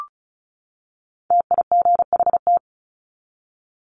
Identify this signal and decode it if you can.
Morse code, heard as a keyed tone, about 34 wpm, 700 Hz. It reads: TIGHT
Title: Top 1751-2000 Words 35wpm